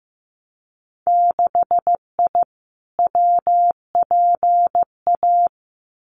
Morse code, heard as a keyed tone, about 15 wpm, 700 Hz. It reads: 6I WPA
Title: Winter Field Day Exchanges Encore 15wpm